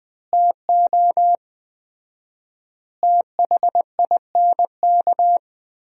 Morse code, heard as a keyed tone, about 20 wpm, 700 Hz. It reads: TO THINK
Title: 4x Word-Spacing - Sentences from Top 400 Words - 20wpm